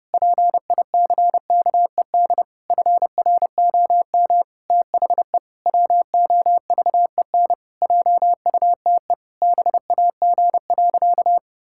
Morse code, heard as a keyed tone, about 30 wpm, 700 Hz. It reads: PICKED FROM THE WOVEN JUTE BAG.
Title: Pangram Challenge v2 - 30wpm